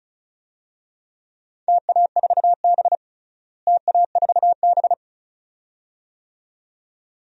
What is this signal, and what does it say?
Morse code, heard as a keyed tone, about 34 wpm, 700 Hz. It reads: TA4B TA4B
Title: DXCC-100 Call Signs - 4 Characters - Repeated 2x - 35wpm